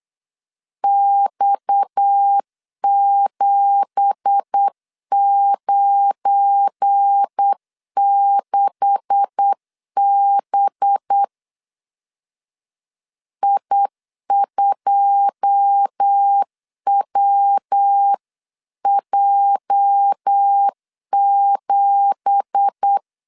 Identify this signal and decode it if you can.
Morse code, heard as a keyed tone, about 8 wpm, 800 Hz. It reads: X796B I2WJ7
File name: 8wpm